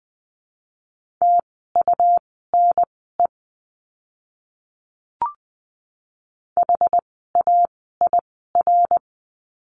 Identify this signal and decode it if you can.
Morse code, heard as a keyed tone, about 20 wpm, spaced as for 15 wpm, 700 Hz. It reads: TUNE HAIR
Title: 101 Lesson 17f - Letter u - Words 20wpm